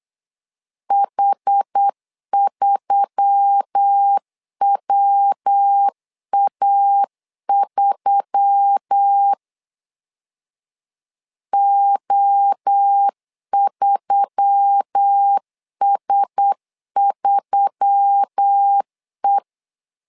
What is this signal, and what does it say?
Morse code, heard as a keyed tone, about 8 wpm, 800 Hz. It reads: H3WA3 O3S3E
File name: LEZIONI / 4 Q1S3 8wpm 3037